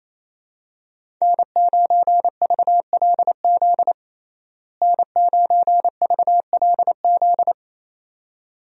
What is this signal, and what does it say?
Morse code, heard as a keyed tone, about 28 wpm, 700 Hz. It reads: N9VLZ N9VLZ
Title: USA Call Signs - 5 Characters - Repeated 2x - 28wpm